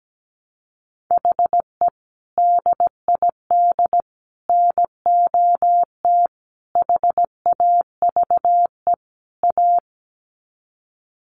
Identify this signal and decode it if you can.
Morse code, heard as a keyed tone, about 17 wpm, 700 Hz. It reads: HE DID NOT HAVE A
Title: Sets of 5 Words Encore 17wpm